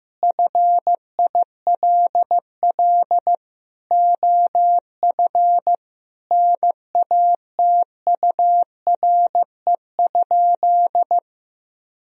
Morse code, heard as a keyed tone, about 15 wpm, 700 Hz. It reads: FILL OF NATURE?
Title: Sentences from Top 1000 Words - Review Part 3 - 15wpm